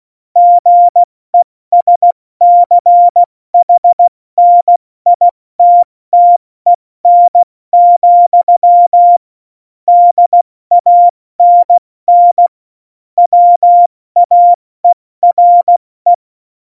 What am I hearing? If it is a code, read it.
Morse code, heard as a keyed tone, about 16 wpm, spaced as for 14 wpm, 700 Hz. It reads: GESCHNITTEN, DANN WAERE
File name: Morsekurs / Witze / Witz 29. September 2020, 70 F80